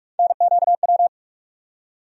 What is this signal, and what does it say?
Morse code, heard as a keyed tone, about 47 wpm, 700 Hz. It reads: NQW
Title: USA Call Sign Postfixes - 3 Letters 45wpm